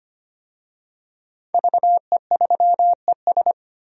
Morse code, heard as a keyed tone, about 25 wpm, 700 Hz. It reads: VE3ES
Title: DXCC-100 Call Signs - 5 Characters - 25wpm